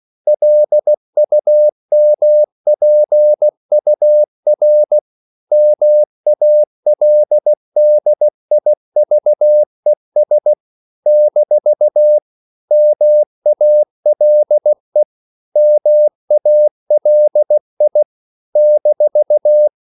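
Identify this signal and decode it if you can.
Morse code, heard as a keyed tone, about 16 wpm, 600 Hz. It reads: LUMPUR MALDIVES - MALE MALI -
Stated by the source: CW practice of national capitals at 16 words per minute, from ON6ZQ.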